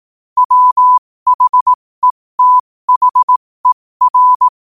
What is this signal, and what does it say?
Morse code, heard as a keyed tone, about 18 wpm, spaced as for 15 wpm, 1000 Hz. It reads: WHETHER
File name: qotdfiles / 15wpm